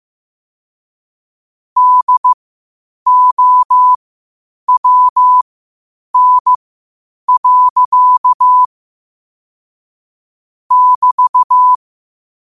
Morse code, heard as a keyed tone, about 15 wpm, spaced as for 8 wpm, 1000 Hz. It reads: DOWN. <BT>
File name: qotdfiles / 8wpm